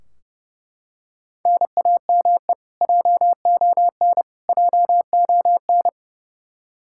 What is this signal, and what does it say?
Morse code, heard as a keyed tone, about 30 wpm, 700 Hz. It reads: NAME JON JON
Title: K1USN SST QSO Practice 30wpm